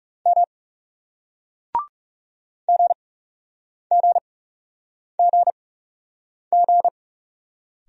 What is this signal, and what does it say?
Morse code, heard as a keyed tone, about 40 wpm, spaced as for 8 wpm, 700 Hz. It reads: M GGGG